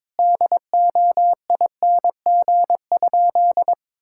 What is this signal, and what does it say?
Morse code, heard as a keyed tone, about 21 wpm, 700 Hz. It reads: DOING?
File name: practice / 22wpm / Sentences with 3-Word Phrases - 22wpm